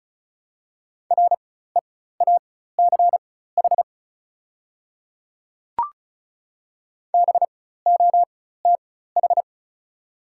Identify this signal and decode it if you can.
Morse code, heard as a keyed tone, about 36 wpm, spaced as for 16 wpm, 690 Hz. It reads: REACH BOTH
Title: 4x Character Spacing - Top 500 Words - 35wpm